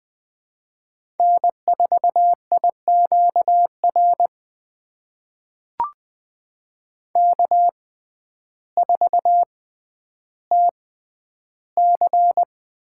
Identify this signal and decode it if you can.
Morse code, heard as a keyed tone, about 20 wpm, spaced as for 11 wpm, 700 Hz. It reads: N4IQR K4TC
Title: USA Call Signs - 6x Character Spacing - 20wpm